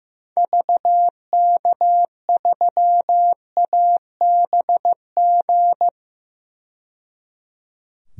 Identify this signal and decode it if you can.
Morse code, heard as a keyed tone, about 15 wpm, 700 Hz. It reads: VK3ABG
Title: DXCC-100 Call Signs - 6 Characters - 15wpm